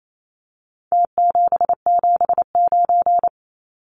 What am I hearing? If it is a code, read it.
Morse code, heard as a keyed tone, about 28 wpm, 700 Hz. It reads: T779
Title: Numbers - Contest Style - 4-Digit 28wpm